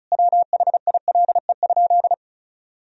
Morse code, heard as a keyed tone, about 36 wpm, 700 Hz. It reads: WHILE?
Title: Sentences from Top 1000 Words - Review Part 1 - 35wpm